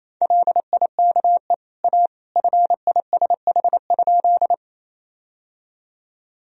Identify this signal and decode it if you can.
Morse code, heard as a keyed tone, about 28 wpm, 700 Hz. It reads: LIKE A FISH?
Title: Sentences from Top 1000 Words - Review Part 1 - 28wpm